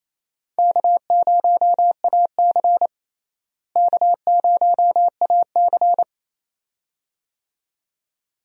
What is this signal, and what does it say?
Morse code, heard as a keyed tone, about 28 wpm, 700 Hz. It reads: K0AC K0AC
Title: USA Call Signs - 4 Characters - Repeated 2x - 28wpm